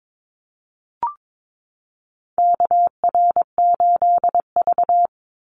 Morse code, heard as a keyed tone, about 22 wpm, 700 Hz. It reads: KR8V
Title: Call Signs - 22wpm